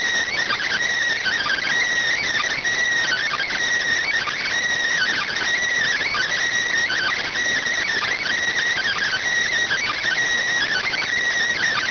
Emergency Alert System (SAME) tones (header bursts), heard as harsh buzzing bursts. It is heard from 1.2 to 11.0 s.